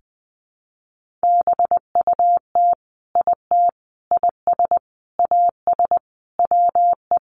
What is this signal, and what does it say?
Morse code, heard as a keyed tone, about 20 wpm, 700 Hz. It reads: BUT IT IS AS WE
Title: Sentences from Top 1500 Words 20wpm